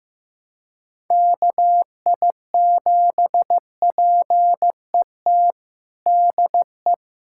Morse code, heard as a keyed tone, about 15 wpm, 700 Hz. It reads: KI7PET DE
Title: USA Callsign DE Callsign - 6 Characters - 15wpm